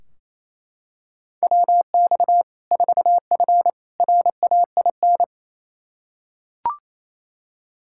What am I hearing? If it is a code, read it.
Morse code, heard as a keyed tone, about 28 wpm, 700 Hz. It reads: WX 4F RAIN
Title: Practice QSO - Exchange 2 - Info - 28wpm